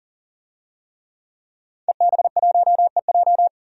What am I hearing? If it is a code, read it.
Morse code, heard as a keyed tone, about 40 wpm, 700 Hz. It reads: EB1EJ
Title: DXCC-100 Call Signs - 5 Characters - 40wpm